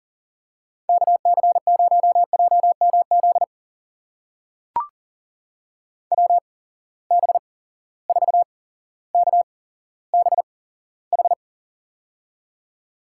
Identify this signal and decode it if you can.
Morse code, heard as a keyed tone, about 40 wpm, spaced as for 17 wpm, 700 Hz. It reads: KC0JMZ WB4KBH
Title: USA Call Signs - 6 Characters - 8x Character Spacing - 40wpm